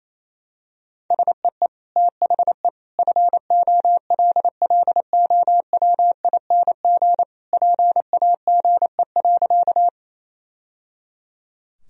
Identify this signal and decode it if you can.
Morse code, heard as a keyed tone, about 28 wpm, 700 Hz. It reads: SEE THE FOLLOWING PAGE.
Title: Sentences from Top 1000 Words - Review Part 1 - 28wpm